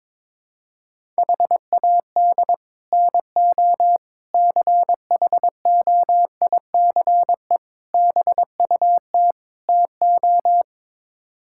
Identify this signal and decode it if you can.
Morse code, heard as a keyed tone, about 21 wpm, 700 Hz. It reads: HAD NO CHOICE BUT TO
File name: practice / 22wpm / Build A Phrase - 5-Word Phrases - 22wpm